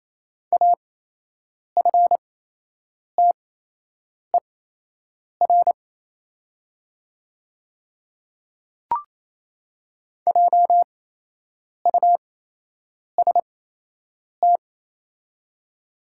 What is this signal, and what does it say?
Morse code, heard as a keyed tone, about 28 wpm, spaced as for 8 wpm, 700 Hz. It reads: AFTER JUST
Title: 8x Character Spacing - Top 200 Words - 28wpm